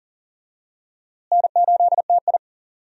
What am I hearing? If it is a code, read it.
Morse code, heard as a keyed tone, about 39 wpm, 700 Hz. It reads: N8TI
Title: USA Call Signs - 4 Characters - Repeated 2x - 40wpm